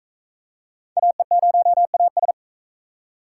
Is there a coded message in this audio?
Yes: Morse code from 1.0 to 2.3 s, heard as a keyed tone, about 39 wpm, 700 Hz. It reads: AE0AS